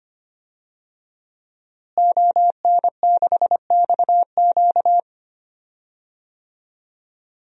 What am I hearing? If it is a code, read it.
Morse code, heard as a keyed tone, about 25 wpm, 700 Hz. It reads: ON6XQ
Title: DXCC-100 Call Signs - 5 Characters - Repeated 2x - 25wpm